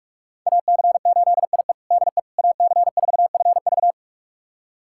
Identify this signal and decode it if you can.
Morse code, heard as a keyed tone, about 45 wpm, 700 Hz. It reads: AC8IE DE AC4FV
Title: USA Callsign DE Callsign - 5 Characters - 45wpm